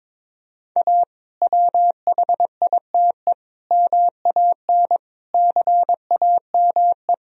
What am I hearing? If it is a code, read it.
Morse code, heard as a keyed tone, about 22 wpm, 700 Hz. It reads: A WHITE MAN CAME
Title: Sentences from Top 1000 Words - Review Part 1 - 22wpm